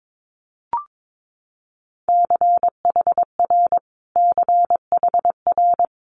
Morse code, heard as a keyed tone, about 22 wpm, 700 Hz. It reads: CHR CHR
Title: Multi-letter Phonemes Repeated 3x 22wpm